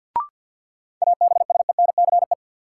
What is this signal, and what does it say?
Morse code, heard as a keyed tone, about 50 wpm, 700 Hz. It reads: ABSENCE
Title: Sets of 4 Words - New Word Drill - 50wpm